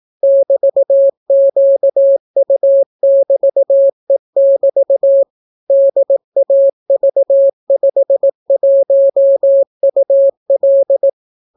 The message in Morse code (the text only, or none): <BT>QU<BT>E<BT> DAV51UL